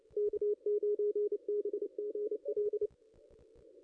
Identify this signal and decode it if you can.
Morse code, heard as a keyed tone, about 28 wpm, 410 Hz. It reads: K9BGL
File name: contests / ss-cw-k5tr-op-n5rz -0500